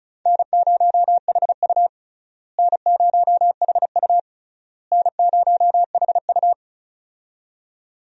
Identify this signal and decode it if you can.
Morse code, heard as a keyed tone, about 36 wpm, 700 Hz. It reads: N0HU N0HU N0HU
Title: USA Call Signs - 4 Characters - Repeated 3x - 35wpm